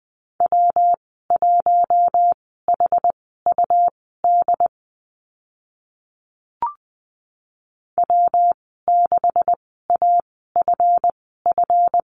Morse code, heard as a keyed tone, about 20 wpm, spaced as for 14 wpm, 700 Hz. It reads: W1HUD W6AFF
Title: 219 Lesson 35g - Number 0 - Call Signs 20wpm